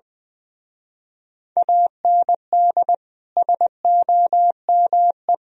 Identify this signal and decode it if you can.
Morse code, heard as a keyed tone, about 24 wpm, 700 Hz. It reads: AND SOME
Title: Speed-Racing - Sets of 4 Words 20wpm